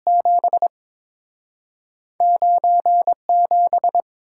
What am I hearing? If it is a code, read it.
Morse code, heard as a keyed tone, about 30 wpm, 700 Hz. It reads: 7 9 7